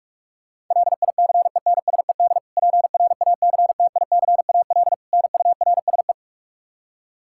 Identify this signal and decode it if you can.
Morse code, heard as a keyed tone, about 46 wpm, 700 Hz. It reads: LICENSED PRACTICAL NURSE
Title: Practice QSO - Exchange 3 - More Info - 45wpm